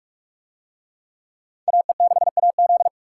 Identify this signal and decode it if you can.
Morse code, heard as a keyed tone, about 44 wpm, 700 Hz. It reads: AE6AZ